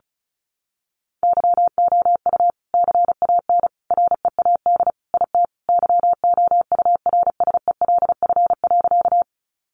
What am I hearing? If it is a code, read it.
Morse code, heard as a keyed tone, about 34 wpm, 700 Hz. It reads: YOU CAN READ IT YOURSELF.
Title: Sentences from Top 600 Words 35wpm